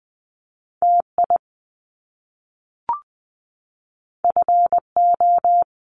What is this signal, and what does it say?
Morse code, heard as a keyed tone, about 20 wpm, 700 Hz. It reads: TI FO
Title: Top Two Letter Combinations in Words 20wpm